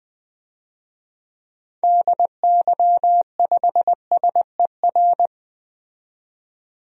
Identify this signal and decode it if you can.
Morse code, heard as a keyed tone, about 20 wpm, 700 Hz. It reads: DY5SER
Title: DXCC-100 Call Signs - 6 Characters - 20wpm